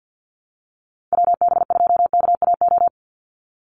Morse code, heard as a keyed tone, about 49 wpm, 700 Hz. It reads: WB2KAO
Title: Call Signs Repeated 3x - 50wpm